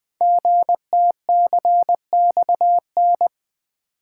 Morse code, heard as a keyed tone, about 20 wpm, 700 Hz. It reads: GTCXN
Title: Code Groups - 5 Letters - 20wpm